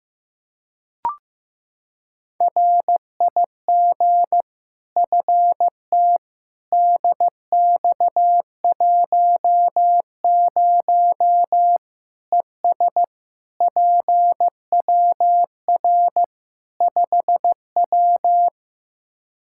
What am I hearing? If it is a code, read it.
Morse code, heard as a keyed tone, about 15 wpm, 700 Hz. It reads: RIG FT DX10 ES PWR 5W
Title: Practice QSO - Exchange 2 - Info - 15wpm